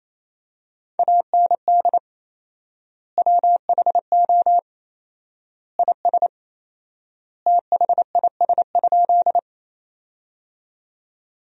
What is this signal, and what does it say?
Morse code, heard as a keyed tone, about 27 wpm, 700 Hz. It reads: AND WHO IS THIS?
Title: 4x Word-Spacing - Sentences from Top 100 Words - 28wpm